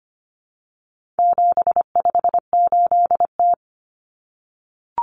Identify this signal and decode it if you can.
Morse code, heard as a keyed tone, about 25 wpm, 700 Hz. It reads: 758T
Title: Common Cut Numbers - 4-Digit 25wpm